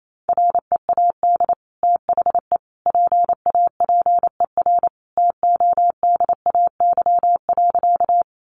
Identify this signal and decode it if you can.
Morse code, heard as a keyed tone, about 28 wpm, 700 Hz. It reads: READ THE PAPER TODAY.
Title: Sentences from Top 600 Words 28wpm